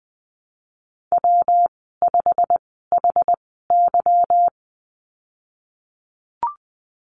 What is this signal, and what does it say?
Morse code, heard as a keyed tone, about 20 wpm, spaced as for 15 wpm, 700 Hz. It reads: W5HY
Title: 147 Lesson 24g - Letter y - Call Signs 20wpm